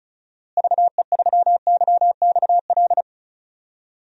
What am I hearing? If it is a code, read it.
Morse code, heard as a keyed tone, about 36 wpm, 700 Hz. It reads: VE3YXL
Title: DXCC-100 Call Signs - 6 Characters - Repeated 2x - 35wpm